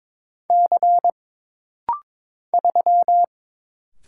Morse code, heard as a keyed tone, about 22 wpm, 700 Hz. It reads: C 3
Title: Single Letter-Number Alt - Rapid-Fire 22wpm